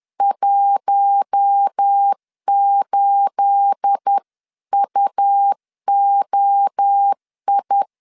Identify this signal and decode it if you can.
Morse code, heard as a keyed tone, about 10 wpm, 800 Hz. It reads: 18UOI
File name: LEZIONI / 10 SIGNS / 10A SLASH 10wpm 2808